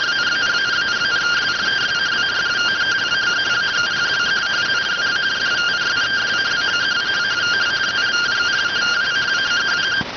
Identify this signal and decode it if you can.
Two-tone teleprinter signal, about 170 Hz shift.